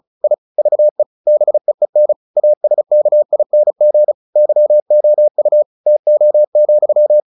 Morse code, heard as a keyed tone, about 34 wpm, 600 Hz. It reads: I VE BEEN ASKING YOU TO,
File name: the old man and the sea / the old man and the sea 1 35wpm